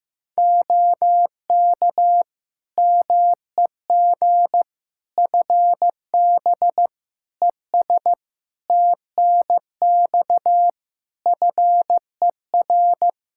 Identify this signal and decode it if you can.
Morse code, heard as a keyed tone, about 15 wpm, 700 Hz. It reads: OK MEG FB ES TNX FER
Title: Practice QSO - Exchange 2 - Info - 15wpm